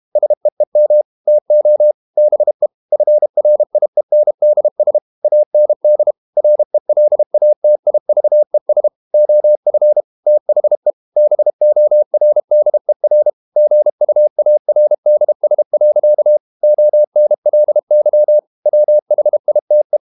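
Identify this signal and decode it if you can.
Morse code, heard as a keyed tone, about 32 wpm, 600 Hz. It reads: SEEM TO BE FRIENDS AND RELATIVES OF THE BORDER GUARDS. ONLY WHITE
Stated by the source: Morse code practice at 32 wpm from CW med Gnister, reading Ken Follett's novel Never in English.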